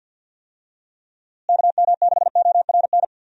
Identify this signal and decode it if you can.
Morse code, heard as a keyed tone, about 51 wpm, 700 Hz. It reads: KM6ORN
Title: USA Call Signs - 6 Characters - Repeated 2x - 50wpm